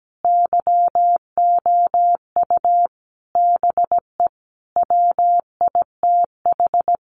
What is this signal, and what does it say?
Morse code, heard as a keyed tone, about 17 wpm, 700 Hz. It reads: YOU BE WITH
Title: Sentences from Top 1000 Words 17wpm